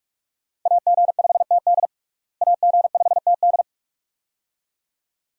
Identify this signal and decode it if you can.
Morse code, heard as a keyed tone, about 43 wpm, 700 Hz. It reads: AG5TD AG5TD
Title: USA Call Signs - 5 Characters - Repeated 2x - 45wpm